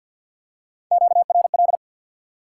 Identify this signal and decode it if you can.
Morse code, heard as a keyed tone, about 48 wpm, 700 Hz. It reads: QRL
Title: Top 180 QSO Elements 50wpm